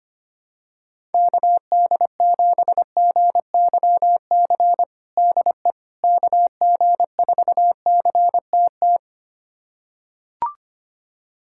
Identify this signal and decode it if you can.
Morse code, heard as a keyed tone, about 25 wpm, 700 Hz. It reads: KD7GYC DE KG4CTT
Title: USA Callsign DE Callsign - 6 Characters - 25wpm